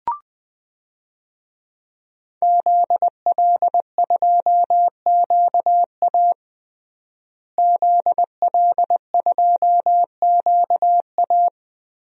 Morse code, heard as a keyed tone, about 20 wpm, 700 Hz. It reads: ZL2QA ZL2QA
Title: DXCC-100 Call Signs - 5 Characters - Repeated 2x - 20wpm